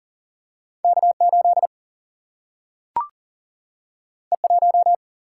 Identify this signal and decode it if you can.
Morse code, heard as a keyed tone, about 40 wpm, 700 Hz. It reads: K8 E1
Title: Single-Letter Single-Number 40wpm